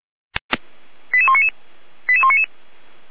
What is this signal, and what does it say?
Five-tone selective call (ZVEI), heard as quick runs of five tones, tones of about 70 ms. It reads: 88199 88199